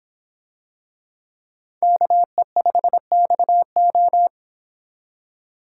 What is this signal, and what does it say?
Morse code, heard as a keyed tone, about 26 wpm, 700 Hz. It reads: KE5XO